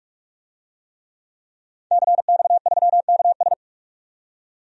Morse code, heard as a keyed tone, about 48 wpm, 700 Hz. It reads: CX3KS
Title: DXCC-100 Call Signs - 5 Characters - 45wpm